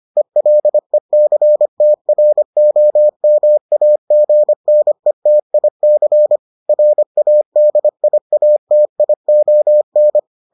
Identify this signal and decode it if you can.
Morse code, heard as a keyed tone, about 25 wpm, 600 Hz. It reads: ELECTROMAGNETIC RADIATION